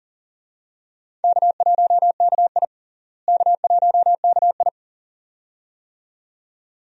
Morse code, heard as a keyed tone, about 40 wpm, 700 Hz. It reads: K1KI K1KI
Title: USA Call Signs - 4 Characters - Repeated 2x - 40wpm